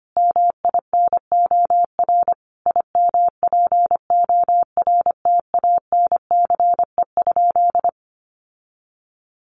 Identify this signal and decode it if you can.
Morse code, heard as a keyed tone, about 25 wpm, 700 Hz. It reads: MINOR IMPORTANCE?
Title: Sentences from Top 1500 Words 25wpm